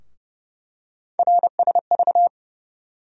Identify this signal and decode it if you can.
Morse code, heard as a keyed tone, about 30 wpm, 700 Hz. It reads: RSV
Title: USA Call Sign Postfixes - 3 Letters 30wpm